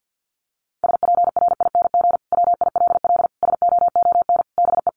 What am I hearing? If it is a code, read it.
Morse code, heard as a keyed tone, about 50 wpm, 700 Hz. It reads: SPRING WILL SOON BE
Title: Sentences from Top 1200 Words 50wpm